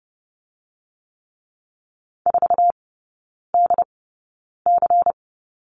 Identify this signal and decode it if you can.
Morse code, heard as a keyed tone, about 30 wpm, spaced as for 9 wpm, 690 Hz. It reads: 4DC
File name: Lesson 17e - Letter u - Three Characters 30wpm